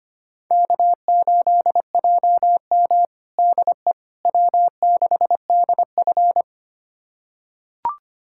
Morse code, heard as a keyed tone, about 25 wpm, 700 Hz. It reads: K8JM DE W6DF
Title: USA Callsign DE Callsign - 4 Characters - 25wpm